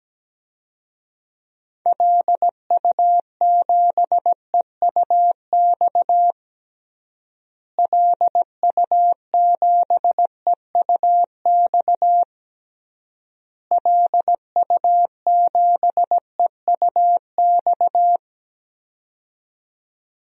Morse code, heard as a keyed tone, about 17 wpm, 700 Hz. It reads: LU7EUX LU7EUX LU7EUX
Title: DXCC-100 Call Signs - 6 Characters - Repeated 3x - 17wpm